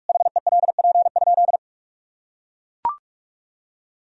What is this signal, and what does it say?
Morse code, heard as a keyed tone, about 41 wpm, 700 Hz. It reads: HELP?
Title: Sentences from Top 200 Words and Abbreviations 45wpm